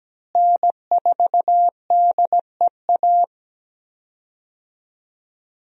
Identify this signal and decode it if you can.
Morse code, heard as a keyed tone, about 17 wpm, 700 Hz. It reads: N4DEA
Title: USA Call Signs - 5 Characters - Repeated 3x - 17wpm